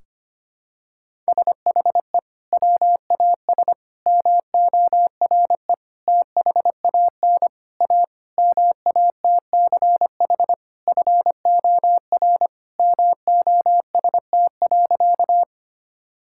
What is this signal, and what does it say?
Morse code, heard as a keyed tone, about 25 wpm, 700 Hz. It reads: SHE WAS MORE THAN A MATCH FOR MOST.
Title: Sentences from Top 1600 Words 25wpm